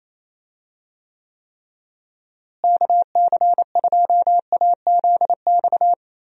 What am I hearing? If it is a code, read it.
Morse code, heard as a keyed tone, about 28 wpm, 700 Hz. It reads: KC2AZX
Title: USA Call Signs - 6 Characters - Repeated 3x - 28wpm